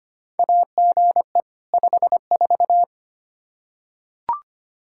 Morse code, heard as a keyed tone, about 25 wpm, 700 Hz. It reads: AGE 54
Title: K1USN SST QSO Practice 25wpm